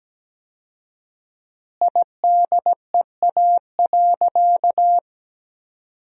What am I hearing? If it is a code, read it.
Morse code, heard as a keyed tone, about 17 wpm, 700 Hz. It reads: IDEA.
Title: 4x Word-Spacing - Sentences from Top 300 Words - 17wpm